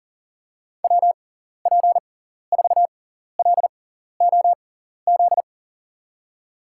Morse code, heard as a keyed tone, about 37 wpm, spaced as for 14 wpm, 700 Hz. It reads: WP4LOZ
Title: USA Call Signs - 6x Character Spacing - 40wpm